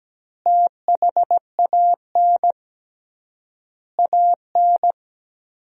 Morse code, heard as a keyed tone, about 17 wpm, 700 Hz. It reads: THAN AN
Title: 3x Word-Spacing - Sentences from Top 500 Words - 17wpm